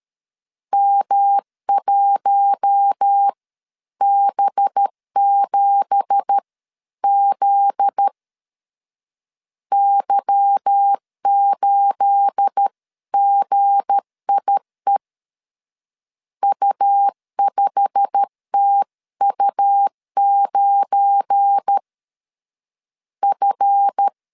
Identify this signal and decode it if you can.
Morse code, heard as a keyed tone, about 13 wpm, spaced as for 11 wpm, 800 Hz. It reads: M1B7Z Y8GIE U5TU9 F
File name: CW / DET / 1 9 1500C S B 12wpm 3819